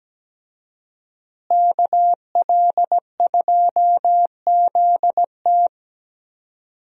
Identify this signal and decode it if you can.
Morse code, heard as a keyed tone, about 17 wpm, 700 Hz. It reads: KL2ZT
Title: USA Call Signs - 5 Characters - Repeated 2x - 17wpm